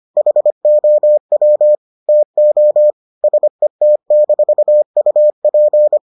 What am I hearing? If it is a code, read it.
Morse code, heard as a keyed tone, about 25 wpm, 600 Hz. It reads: HOW TO SET-UP